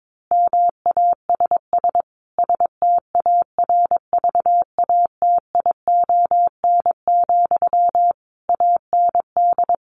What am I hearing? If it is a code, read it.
Morse code, heard as a keyed tone, about 22 wpm, 700 Hz. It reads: MASS STARVATION, AND
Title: A Sound Of Thunder 22wpm